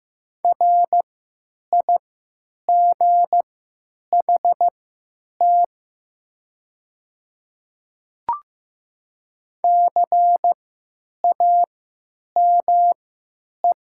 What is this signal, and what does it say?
Morse code, heard as a keyed tone, about 15 wpm, spaced as for 9 wpm, 700 Hz. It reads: RIGHT CAME
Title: 3x Character Spacing - Top 200 Words - 15wpm